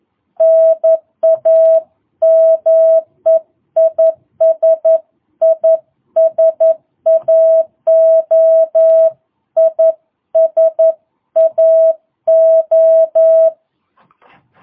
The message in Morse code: NAME IS ISAO ISAO

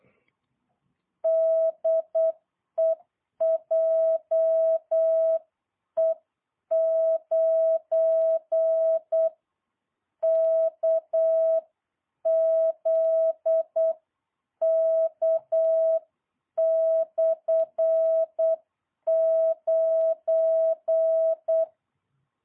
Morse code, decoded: DE JE9KZK/9

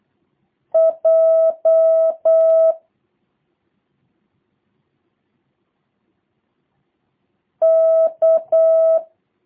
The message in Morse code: J K

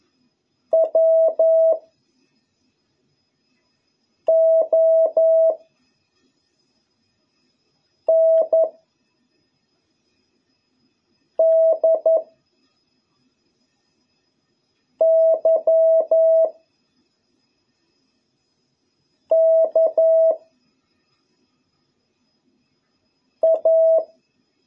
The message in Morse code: W O N D Y K A

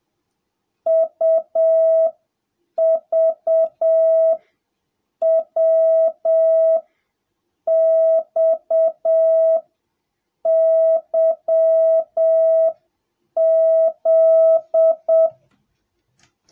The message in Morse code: UVWXYZ

